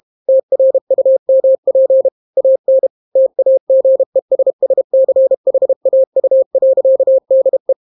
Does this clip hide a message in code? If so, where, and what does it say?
Yes: Morse code from 0.3 to 7.7 s, heard as a keyed tone, about 32 wpm, 530 Hz. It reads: TRUMP AN TAGESSCHAU.DE